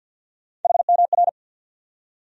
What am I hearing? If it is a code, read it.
Morse code, heard as a keyed tone, about 51 wpm, 700 Hz. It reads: HMR
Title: USA Call Sign Postfixes - 3 Letters 50wpm